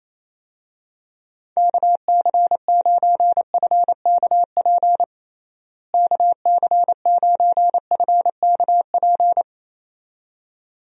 Morse code, heard as a keyed tone, about 27 wpm, 700 Hz. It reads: KC9FKP KC9FKP
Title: USA Call Signs - 6 Characters - Repeated 2x - 28wpm